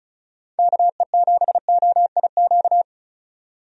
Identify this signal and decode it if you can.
Morse code, heard as a keyed tone, about 34 wpm, 700 Hz. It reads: KE7OIQ
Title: USA Call Signs - 6 Characters - 35wpm